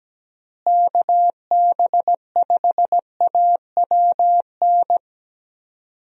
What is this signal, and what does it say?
Morse code, heard as a keyed tone, about 17 wpm, 700 Hz. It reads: KB5AWN
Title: USA Call Signs - 6 Characters - Repeated 3x - 17wpm